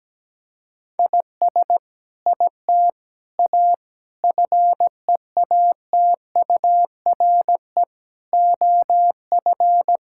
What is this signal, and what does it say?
Morse code, heard as a keyed tone, about 17 wpm, 700 Hz. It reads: IS IT A FEATURE OF
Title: Sentences from Top 1000 Words - Review Part 5 - 17wpm